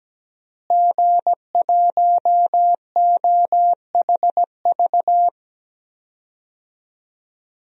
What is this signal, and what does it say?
Morse code, heard as a keyed tone, about 17 wpm, 700 Hz. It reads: G1OHV
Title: DXCC-100 Call Signs - 5 Characters - Repeated 2x - 17wpm